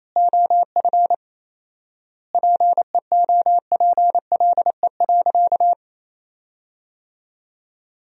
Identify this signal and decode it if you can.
Morse code, heard as a keyed tone, about 28 wpm, 700 Hz. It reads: OF PEOPLE.
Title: 4x Word-Spacing - Sentences from Top 500 Words - 28wpm